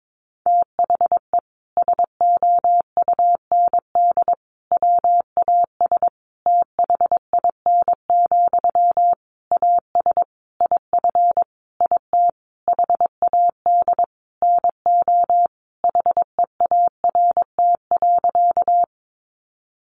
Morse code, heard as a keyed tone, about 22 wpm, 700 Hz. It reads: THE SOUND WAS THIN, AS IF IT HAD NO HEART.
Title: Sentences from Top 1200 Words 22wpm